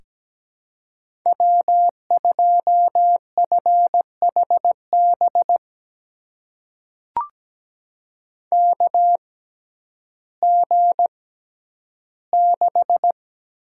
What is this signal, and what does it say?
Morse code, heard as a keyed tone, about 17 wpm, spaced as for 10 wpm, 700 Hz. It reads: W2FHB KG6
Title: USA Call Signs - 5 Characters - 6x Character Spacing - 17wpm